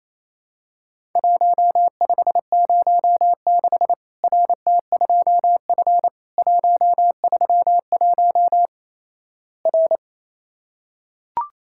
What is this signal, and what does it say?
Morse code, heard as a keyed tone, about 28 wpm, 700 Hz. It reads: 1506 RT2F 131 R
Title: Worked All Europe DX Contest - QTC Exchange - 28wpm